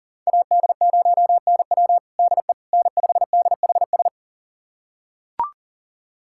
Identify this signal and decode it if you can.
Morse code, heard as a keyed tone, about 37 wpm, 700 Hz. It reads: AD0NW DE N5DHS
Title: USA Callsign DE Callsign - 5 Characters - 40wpm